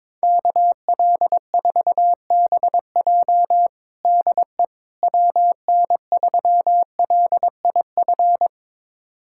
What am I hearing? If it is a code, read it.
Morse code, heard as a keyed tone, about 21 wpm, 700 Hz. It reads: KL4BJ DE WN3LIF
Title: USA Callsign DE Callsign 22wpm